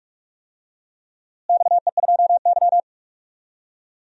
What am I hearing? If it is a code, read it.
Morse code, heard as a keyed tone, about 46 wpm, 700 Hz. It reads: XE2Y
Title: DXCC-100 Call Signs - 4 Characters - Repeated 2x - 45wpm